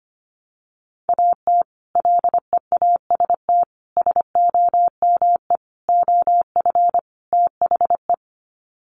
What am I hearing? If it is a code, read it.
Morse code, heard as a keyed tone, about 25 wpm, 700 Hz. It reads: AT LEAST SOME OF THE
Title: Sets of 5 Words Encore 25wpm